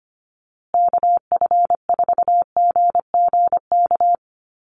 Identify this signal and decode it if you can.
Morse code, heard as a keyed tone, about 25 wpm, 700 Hz. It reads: KF4GGK
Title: USA Call Signs Repeated 2x 25wpm